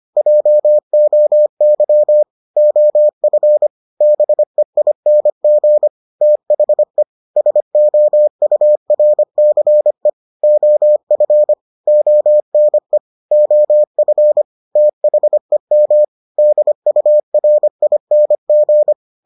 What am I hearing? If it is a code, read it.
Morse code, heard as a keyed tone, about 25 wpm, 600 Hz. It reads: JOY OF BEING THE SOURCE OF ONE OF THEM DURING